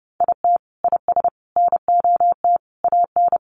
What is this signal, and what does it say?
Morse code, heard as a keyed tone, about 28 wpm, 700 Hz. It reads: IT IS NOT AN
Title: Sentences from Top 1500 Words 30wpm